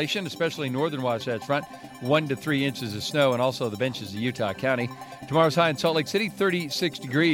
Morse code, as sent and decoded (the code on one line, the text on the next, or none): none